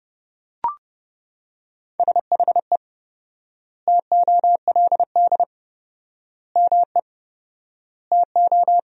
Morse code, heard as a keyed tone, about 30 wpm, 700 Hz. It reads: SHE TOLD ME TO
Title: 4x Word-Spacing - Sentences from Top 400 Words - 30wpm